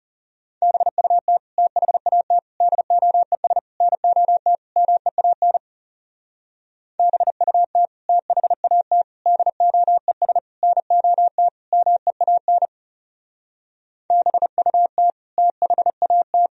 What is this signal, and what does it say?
Morse code, heard as a keyed tone, about 37 wpm, 700 Hz. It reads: BUT THAT DOES NOT MEAN BUT THAT DOES NOT MEAN BUT THAT
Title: Speed-Racing - Sets of 5 Words 30wpm